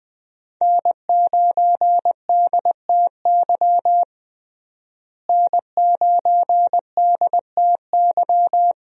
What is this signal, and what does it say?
Morse code, heard as a keyed tone, about 20 wpm, 700 Hz. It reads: N9DTY N9DTY
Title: USA Call Signs - 5 Characters - Repeated 3x - 20wpm